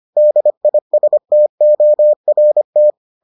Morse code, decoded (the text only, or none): DISTORT